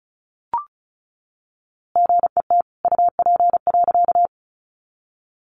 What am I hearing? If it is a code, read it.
Morse code, heard as a keyed tone, about 34 wpm, 700 Hz. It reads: GET UP.
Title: A Sound Of Thunder 35wpm